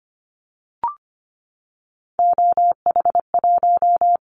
Morse code, heard as a keyed tone, about 25 wpm, 700 Hz. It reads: OH1
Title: Callsign Prefixes for Top 25 DXCC - Repeated 3x - 25wpm